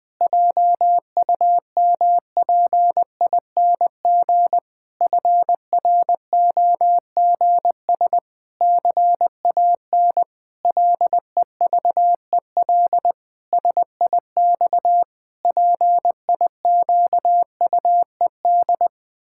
Morse code, heard as a keyed tone, about 20 wpm, 700 Hz. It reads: JUMPING FROGS CAN LEVEL SIX PIQUED
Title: Pangram Challenge v2 - 20wpm